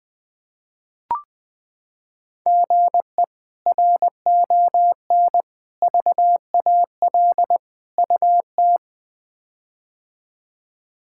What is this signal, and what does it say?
Morse code, heard as a keyed tone, about 20 wpm, 700 Hz. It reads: GE RON VAL UT
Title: K1USN SST QSO Practice - 20wpm